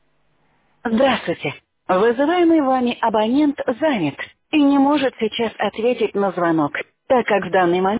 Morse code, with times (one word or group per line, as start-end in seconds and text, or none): none